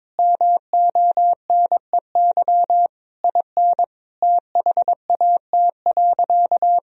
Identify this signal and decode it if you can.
Morse code, heard as a keyed tone, about 21 wpm, 700 Hz. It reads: MONEY IN THAT.
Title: Sentences from Top 1000 Words - Review Part 4 - 22wpm